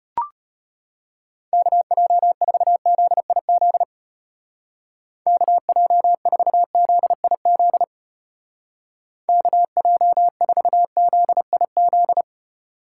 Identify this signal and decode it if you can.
Morse code, heard as a keyed tone, about 32 wpm, 700 Hz. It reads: KJ4ZIZ KJ4ZIZ KJ4ZIZ